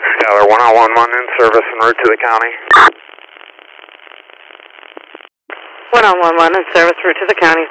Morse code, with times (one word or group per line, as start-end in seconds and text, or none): none